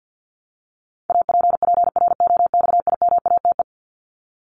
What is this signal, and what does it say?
Morse code, heard as a keyed tone, about 50 wpm, 700 Hz. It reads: APPROXIMATE
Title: LY Suffix - Top 100 Words with Base Words 50wpm